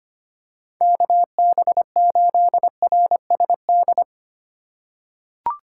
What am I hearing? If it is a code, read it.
Morse code, heard as a keyed tone, about 25 wpm, 690 Hz. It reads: KB8RSD
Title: USA Call Signs - 6 Characters - 2x Character Spacing - 25wpm